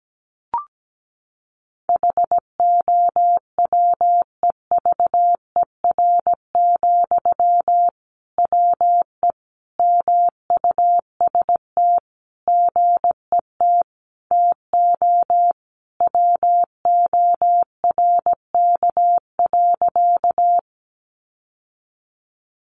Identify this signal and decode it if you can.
Morse code, heard as a keyed tone, about 17 wpm, 700 Hz. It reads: HOWEVER, WE MUST GET TO WORK.
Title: Sentences from Top 600 Words 17wpm